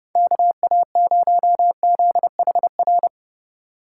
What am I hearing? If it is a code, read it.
Morse code, heard as a keyed tone, about 29 wpm, 700 Hz. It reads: KA0ZHR